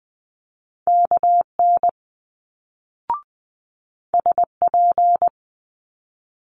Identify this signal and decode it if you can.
Morse code, heard as a keyed tone, about 20 wpm, 700 Hz. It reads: KN SP
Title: Multi-letter Phonemes Quick Fire 20wpm